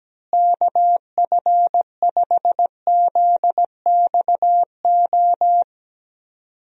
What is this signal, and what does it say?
Morse code, heard as a keyed tone, about 17 wpm, 700 Hz. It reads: KF5ZXO
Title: USA Call Signs - 6 Characters - Repeated 3x - 17wpm